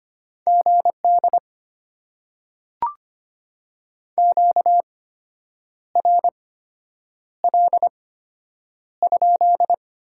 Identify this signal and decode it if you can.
Morse code, heard as a keyed tone, about 25 wpm, spaced as for 8 wpm, 690 Hz. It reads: GD QRL?
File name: Top 90 QSO Elements - 8x Character Spacing - 25wpm